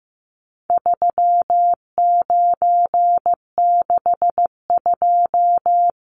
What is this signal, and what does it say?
Morse code, heard as a keyed tone, about 15 wpm, 700 Hz. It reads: 3962
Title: Numbers - 4-Digit 15wpm